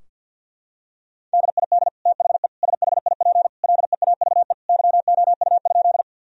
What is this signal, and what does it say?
Morse code, heard as a keyed tone, about 51 wpm, 700 Hz. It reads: DID THE SHIP LEAVE YOU?
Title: Sentences from Top 1000 Words - Review Part 2 - 50wpm